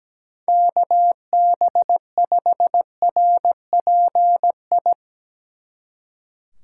Morse code, heard as a keyed tone, about 17 wpm, 700 Hz. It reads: KB5RPI
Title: Speed Racing - USA 2x3 Call Signs 17wpm